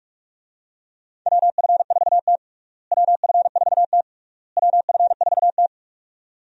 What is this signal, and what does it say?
Morse code, heard as a keyed tone, about 41 wpm, 700 Hz. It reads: WF4T WF4T WF4T
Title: USA Call Signs - 4 Characters - Repeated 3x - 45wpm